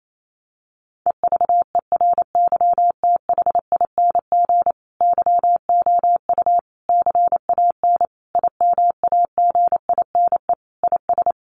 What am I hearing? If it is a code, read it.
Morse code, heard as a keyed tone, about 28 wpm, 700 Hz. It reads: EVERYTHING YOU CAN IMAGINE IS
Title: Sentences from Top 1400 Words 28wpm